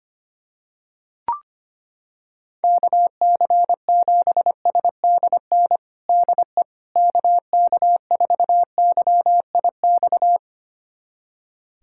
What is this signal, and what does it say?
Morse code, heard as a keyed tone, about 25 wpm, 700 Hz. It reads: KC7SDN DE KK4YIX
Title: USA Callsign DE Callsign - 6 Characters - 25wpm